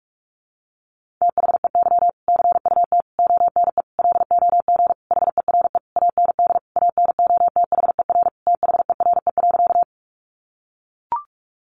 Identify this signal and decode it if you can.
Morse code, heard as a keyed tone, about 44 wpm, 700 Hz. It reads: THEY CUT ONE LOG HERE AND ANOTHER THERE.
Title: Sentences from Top 1400 Words 45wpm